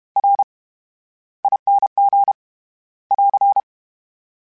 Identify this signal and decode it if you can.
Morse code, heard as a keyed tone, about 32 wpm, 800 Hz. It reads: R ING <AR>
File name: Special INGs / Special INGs-mp3s / VerySlowToFaster Ing @30